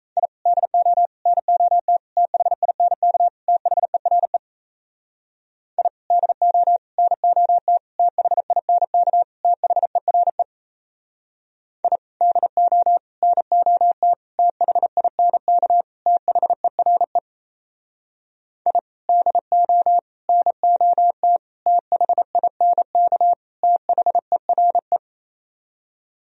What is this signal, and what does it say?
Morse code, heard as a keyed tone, about 36 wpm, 700 Hz. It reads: I DO NOT THINK THERE I DO NOT THINK THERE I DO NOT THINK THERE I DO NOT THINK THERE